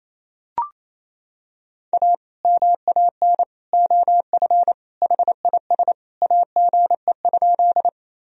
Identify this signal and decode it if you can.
Morse code, heard as a keyed tone, about 28 wpm, 700 Hz. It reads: A MAN OF HIS AGE?
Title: Sentences from Top 1000 Words - Review Part 3 - 28wpm